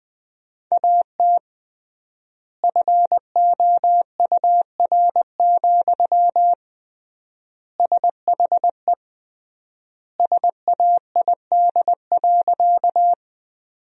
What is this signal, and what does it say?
Morse code, heard as a keyed tone, about 21 wpm, 700 Hz. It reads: AT FOUR, SHE SAID.
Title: 3x Word-Spacing - Sentences from Top 300 Words - 20wpm